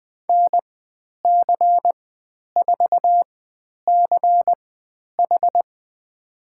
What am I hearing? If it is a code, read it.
Morse code, heard as a keyed tone, about 20 wpm, spaced as for 10 wpm, 700 Hz. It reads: NC4CH
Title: 102 Lesson 17g - Letter u - Call Signs 20wpm